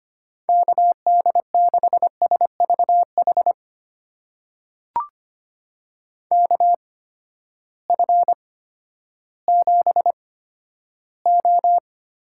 Morse code, heard as a keyed tone, about 25 wpm, spaced as for 12 wpm, 700 Hz. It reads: KD6SVH KF7O
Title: USA Call Signs - 6 Characters - 8x Character Spacing - 25wpm